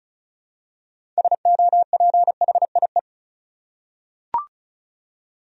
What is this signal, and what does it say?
Morse code, heard as a keyed tone, about 34 wpm, 700 Hz. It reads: SOPHIE
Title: USA Common Names - 35wpm